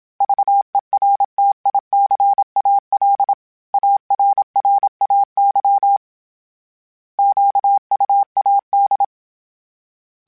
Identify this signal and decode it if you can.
Morse code, heard as a keyed tone, about 27 wpm, 800 Hz. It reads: VERTICAL ARRAY QUAD
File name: fromSMs / FromSMs mp3s / Antennas@25